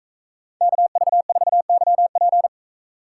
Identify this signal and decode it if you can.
Morse code, heard as a keyed tone, about 38 wpm, 700 Hz. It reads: KV4YP